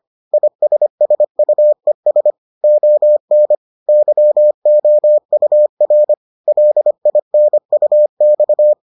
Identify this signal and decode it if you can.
Morse code, heard as a keyed tone, about 25 wpm, 600 Hz. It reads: ISSUES ON YOUR LINUX